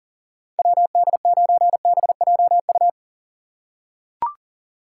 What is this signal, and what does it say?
Morse code, heard as a keyed tone, about 40 wpm, 700 Hz. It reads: WD9BJU
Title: USA Call Signs - 6 Characters - 4x Character Spacing - 40wpm